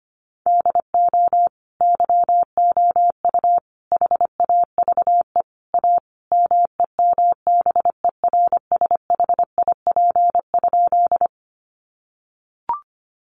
Morse code, heard as a keyed tone, about 25 wpm, 700 Hz. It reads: DO YOU HAVE A MEMBERSHIP?
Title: Sentences from Top 1400 Words 25wpm